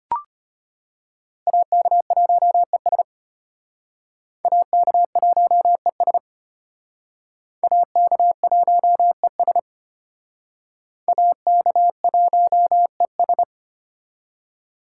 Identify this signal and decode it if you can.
Morse code, heard as a keyed tone, about 34 wpm, 700 Hz. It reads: AK1ES AK1ES AK1ES AK1ES